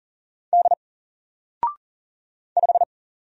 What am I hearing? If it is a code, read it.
Morse code, heard as a keyed tone, about 40 wpm, 700 Hz. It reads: D 5
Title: Single Letter-Number 40wpm